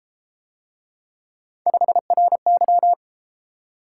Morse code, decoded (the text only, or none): 5RY